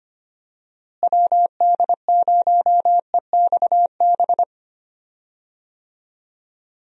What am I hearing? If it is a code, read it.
Morse code, heard as a keyed tone, about 25 wpm, 700 Hz. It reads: WD0EXB
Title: USA Call Signs - 6 Characters - Repeated 2x - 25wpm